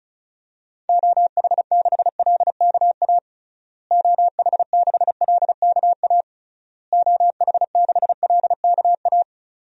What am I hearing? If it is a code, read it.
Morse code, heard as a keyed tone, about 36 wpm, 700 Hz. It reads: OH6LKA OH6LKA OH6LKA
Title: DXCC-100 Call Signs - 6 Characters - Repeated 3x - 35wpm